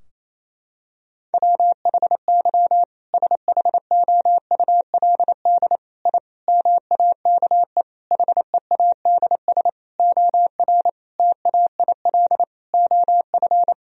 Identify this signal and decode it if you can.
Morse code, heard as a keyed tone, about 28 wpm, 700 Hz. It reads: WHY SHOULD I MAKE HEADS OR TAIL OF
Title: Sentences from Top 1000 Words - Review Part 3 - 28wpm